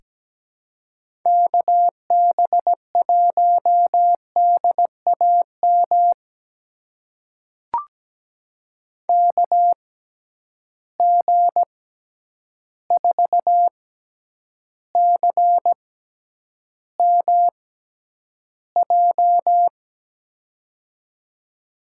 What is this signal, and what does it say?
Morse code, heard as a keyed tone, about 17 wpm, spaced as for 9 wpm, 700 Hz. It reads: KB1DAM KG4CMJ
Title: USA Call Signs - 6 Characters - 6x Character Spacing - 17wpm